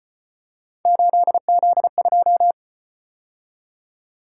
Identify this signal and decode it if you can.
Morse code, heard as a keyed tone, about 33 wpm, 700 Hz. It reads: 8Z2